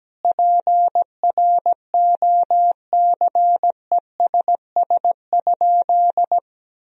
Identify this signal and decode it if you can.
Morse code, heard as a keyed tone, about 17 wpm, 700 Hz. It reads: PROCESS?
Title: Sentences from Top 1000 Words - Review Part 3 - 17wpm